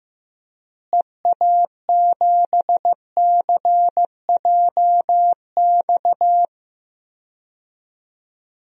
Morse code, heard as a keyed tone, about 15 wpm, 700 Hz. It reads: EA7CJX
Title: DXCC-100 Call Signs - 6 Characters - Repeated 3x - 15wpm